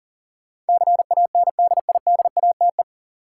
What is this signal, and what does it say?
Morse code, heard as a keyed tone, about 40 wpm, 700 Hz. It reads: CANDIDATE
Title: Sentences from Top 1600 Words - New Word Drill - 40wpm